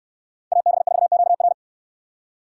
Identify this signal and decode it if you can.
Morse code, heard as a keyed tone, about 66 wpm, 700 Hz. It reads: AB4QR